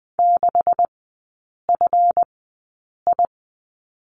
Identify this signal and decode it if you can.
Morse code, heard as a keyed tone, about 20 wpm, spaced as for 8 wpm, 690 Hz. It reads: 6FI
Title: 172 Lesson 28e - Number 9 - Three Characters 20wpm